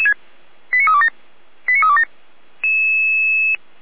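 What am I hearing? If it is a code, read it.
Five-tone selective call (ZVEI), heard as quick runs of five tones, tones of about 70 ms. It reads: 89327 89327
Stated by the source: ZVEI five-tone call 89333 received by SDR.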